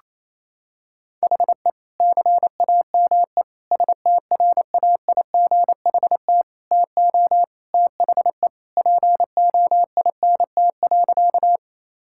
Morse code, heard as a keyed tone, about 28 wpm, 700 Hz. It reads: HE CAME STRAIGHT TO THE POINT.
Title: Sentences from Top 1000 Words - Review Part 6 - 28wpm